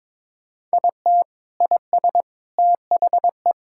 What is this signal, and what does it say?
Morse code, heard as a keyed tone, about 22 wpm, 700 Hz. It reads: IT IS THE
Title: Sentences from Top 1000 Words - Review Part 5 - 22wpm